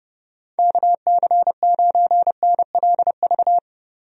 Morse code, heard as a keyed tone, about 32 wpm, 700 Hz. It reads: KC9NLV
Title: USA Call Signs - 6 Characters - Repeated 3x - 30wpm